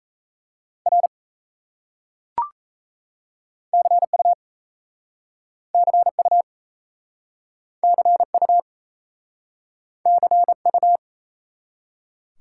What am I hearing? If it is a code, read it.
Morse code, heard as a keyed tone, about 37 wpm, 700 Hz. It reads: R CU CU CU CU